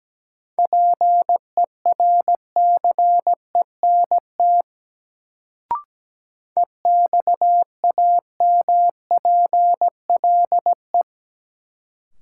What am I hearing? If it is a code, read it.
Morse code, heard as a keyed tone, about 17 wpm, 700 Hz. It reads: PERCENT EXAMPLE
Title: Sets of 2 Words - New Word Drill - 17wpm